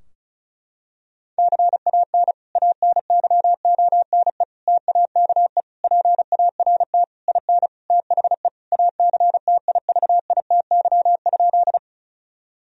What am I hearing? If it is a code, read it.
Morse code, heard as a keyed tone, about 36 wpm, 700 Hz. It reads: CAN ANYONE TAKE PART IN THE ACTIVITY?
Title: Sentences from Top 1000 Words - Review Part 4 - 35wpm